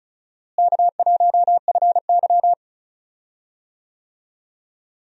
Morse code, heard as a keyed tone, about 34 wpm, 700 Hz. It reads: K1FY
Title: USA Call Signs - 4 Characters - Repeated 2x - 35wpm